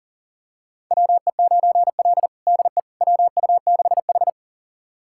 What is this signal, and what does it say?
Morse code, heard as a keyed tone, about 40 wpm, 700 Hz. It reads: WE9L DE WU6H
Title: USA Callsign DE Callsign - 4 Characters - 40wpm